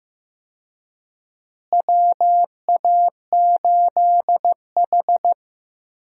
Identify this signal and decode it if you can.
Morse code, heard as a keyed tone, about 15 wpm, 700 Hz. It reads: WA8H
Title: USA Call Signs - 4 Characters - Repeated 3x - 15wpm